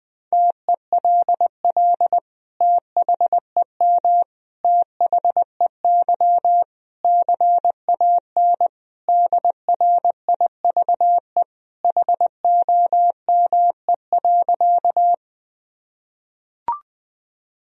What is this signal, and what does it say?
Morse code, heard as a keyed tone, about 20 wpm, 700 Hz. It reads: TELL THEM THEY CAN DRIVE HOME.
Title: Sentences from Top 1000 Words - Review Part 2 - 20wpm